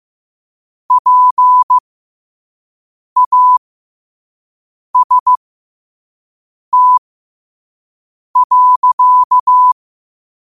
Morse code, heard as a keyed tone, about 15 wpm, spaced as for 5 wpm, 1000 Hz. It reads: PAST.
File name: qotdfiles / 5wpm